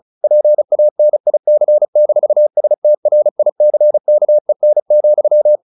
Morse code, heard as a keyed tone, about 34 wpm, 600 Hz. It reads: PANIC-STRICKEN,
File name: the old man and the sea / the old man and the sea 1 35wpm